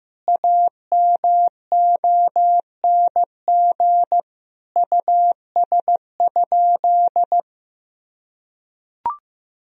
Morse code, heard as a keyed tone, about 16 wpm, 700 Hz. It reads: AMONG US?
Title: Sentences from Top 1000 Words - Review Part 3 - 15wpm